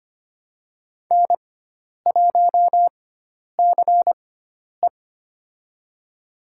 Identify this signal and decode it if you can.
Morse code, heard as a keyed tone, about 25 wpm, spaced as for 10 wpm, 700 Hz. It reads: N1CE
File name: Lesson 16g - Letter c - Call Signs 25wpm